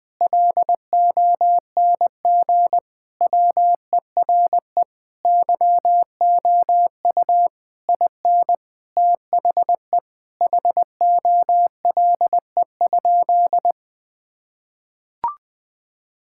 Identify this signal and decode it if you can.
Morse code, heard as a keyed tone, about 20 wpm, 700 Hz. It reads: LONG WERE YOU IN THE HOLE?
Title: Sentences from Top 1600 Words 20wpm